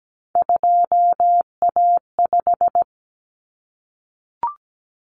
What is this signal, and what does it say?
Morse code, heard as a keyed tone, about 17 wpm, 700 Hz. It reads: 2A5
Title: Common Cut Numbers - 3-Digit 17wpm